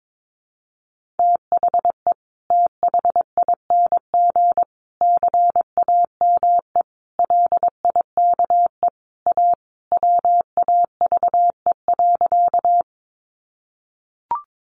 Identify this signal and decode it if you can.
Morse code, heard as a keyed tone, about 22 wpm, 700 Hz. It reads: THE THING CAME LIKE A WAVE.
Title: Sentences from Top 1400 Words 22wpm